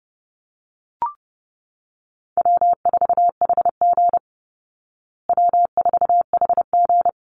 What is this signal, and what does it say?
Morse code, heard as a keyed tone, about 30 wpm, 700 Hz. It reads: W4HG W4HG
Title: Easy Call Signs Repeated 2x - 30wpm